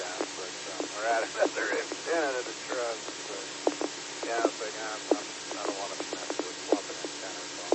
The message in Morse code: none

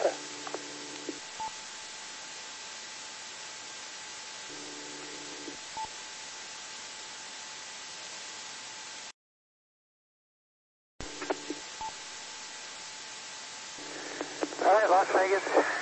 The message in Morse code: none

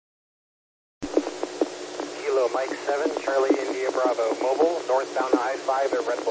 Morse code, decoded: none

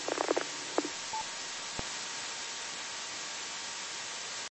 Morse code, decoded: none

